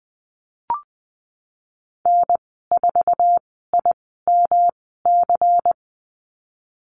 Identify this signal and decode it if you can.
Morse code, heard as a keyed tone, about 20 wpm, spaced as for 15 wpm, 700 Hz. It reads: N4IMC
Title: 140 Lesson 23g - Letter f - Call Signs 20wpm